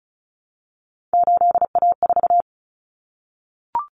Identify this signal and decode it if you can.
Morse code, heard as a keyed tone, about 34 wpm, 700 Hz. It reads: 8A4
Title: Common Cut Numbers - 3-Digit 35wpm